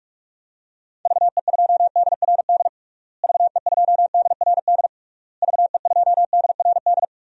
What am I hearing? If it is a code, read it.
Morse code, heard as a keyed tone, about 46 wpm, 700 Hz. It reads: VE2DRD VE2DRD VE2DRD
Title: DXCC-100 Call Signs - 6 Characters - Repeated 3x - 45wpm